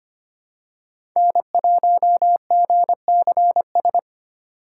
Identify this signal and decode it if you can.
Morse code, heard as a keyed tone, about 25 wpm, 700 Hz. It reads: N1GCS
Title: USA Call Signs - 6x Character Spacing - 25wpm